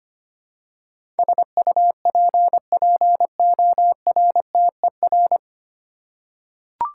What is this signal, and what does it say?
Morse code, heard as a keyed tone, about 25 wpm, 700 Hz. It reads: SUPPORTER
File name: Nine Letter Words - 25wpm